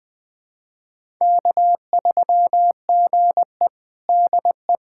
Morse code, heard as a keyed tone, about 20 wpm, 700 Hz. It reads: K3GE DE
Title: USA Callsign DE Callsign - 4 Characters - 20wpm